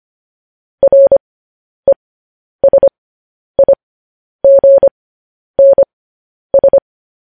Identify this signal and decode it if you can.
Morse code, heard as a keyed tone, about 25 wpm, spaced as for 10 wpm, 550 Hz. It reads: RESIGNS